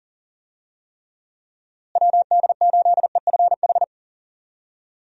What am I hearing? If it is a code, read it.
Morse code, heard as a keyed tone, about 40 wpm, 700 Hz. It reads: WD8EFH
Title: USA Call Signs - 6 Characters - Repeated 3x - 40wpm